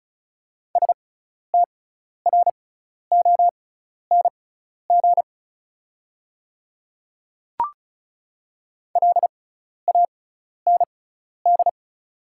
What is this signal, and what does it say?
Morse code, heard as a keyed tone, about 36 wpm, spaced as for 12 wpm, 690 Hz. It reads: STRONG LAND
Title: 6x Character Spacing - Top 500 Words - 35wpm